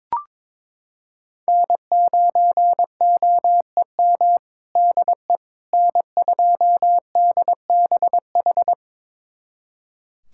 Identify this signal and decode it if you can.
Morse code, heard as a keyed tone, about 22 wpm, 700 Hz. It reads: N9OEM DE N2DBH
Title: USA Callsign DE Callsign - 5 Characters - 22wpm